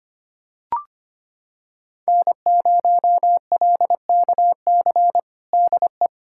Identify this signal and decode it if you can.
Morse code, heard as a keyed tone, about 25 wpm, 700 Hz. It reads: N0LKC DE
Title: USA Callsign DE Callsign - 5 Characters - 25wpm